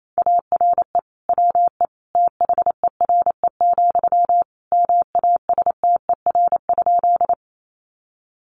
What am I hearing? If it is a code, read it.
Morse code, heard as a keyed tone, about 28 wpm, 700 Hz. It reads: ARE WE THERE, MASTER?
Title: Sentences from Top 1300 Words 28wpm